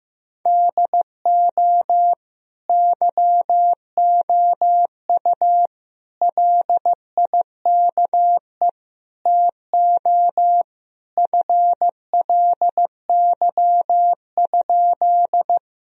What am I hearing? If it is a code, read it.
Morse code, heard as a keyed tone, about 15 wpm, 700 Hz. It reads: DO YOU LIKE TO FLY?
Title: Sentences from Top 1000 Words - Review Part 2 - 15wpm